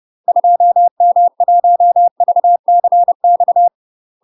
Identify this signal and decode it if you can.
Morse code, heard as a keyed tone, about 30 wpm, 700 Hz. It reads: 2M1VCX
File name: UK MASTER QSOs TRAINING/30WPM